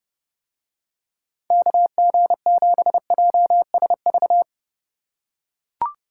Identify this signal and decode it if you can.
Morse code, heard as a keyed tone, about 30 wpm, 700 Hz. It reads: KG7JSV